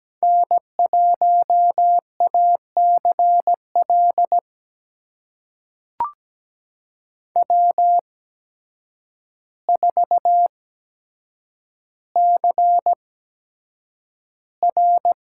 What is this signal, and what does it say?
Morse code, heard as a keyed tone, about 17 wpm, spaced as for 8 wpm, 700 Hz. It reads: N1ACL W4CR
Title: USA Call Signs - 8x Character Spacing - 17wpm